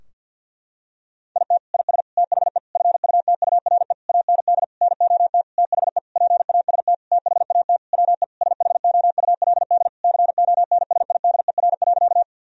Morse code, heard as a keyed tone, about 50 wpm, 700 Hz. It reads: IT IS THE FUTURE AND NOT THE PAST THAT WE SHOULD CONSIDER.
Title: Sentences from Top 1000 Words - Review Part 4 - 50wpm